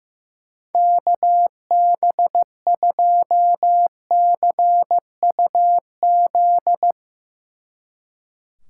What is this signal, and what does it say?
Morse code, heard as a keyed tone, about 15 wpm, 700 Hz. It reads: KB2CUZ
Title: Speed Racing - USA 2x3 Call Signs 15wpm